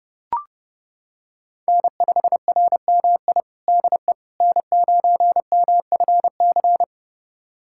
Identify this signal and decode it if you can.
Morse code, heard as a keyed tone, about 30 wpm, 700 Hz. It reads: N5RMI DE N9MFC
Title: USA Callsign DE Callsign - 5 Characters - 30wpm